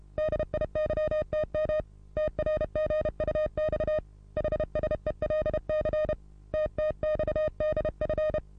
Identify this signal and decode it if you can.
Morse code, heard as a keyed tone, about 32 wpm, 620 Hz. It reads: DIYTM TRGUX HSELC TTXDF
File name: repos / Morsekurs / DJ4UF Morsen27